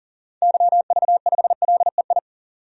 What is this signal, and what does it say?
Morse code, heard as a keyed tone, about 40 wpm, 700 Hz. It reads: YV5LEI
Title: DXCC-100 Call Signs - 6 Characters - 40wpm